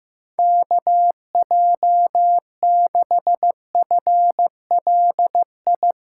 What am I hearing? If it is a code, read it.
Morse code, heard as a keyed tone, about 15 wpm, 700 Hz. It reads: KJ6FLI
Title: USA Callsign DE Callsign - 6 Characters - 15wpm